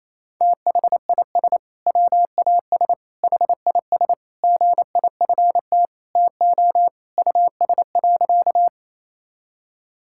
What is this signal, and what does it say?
Morse code, heard as a keyed tone, about 28 wpm, 700 Hz. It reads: THIS WAS HIS GIFT TO US.
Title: Sentences from Top 1000 Words - Review Part 6 - 28wpm